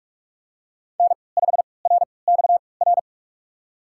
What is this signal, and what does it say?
Morse code, heard as a keyed tone, about 42 wpm, spaced as for 25 wpm, 700 Hz. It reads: N5RXR
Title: USA Call Signs - 5 Characters - 3x Character Spacing - 45wpm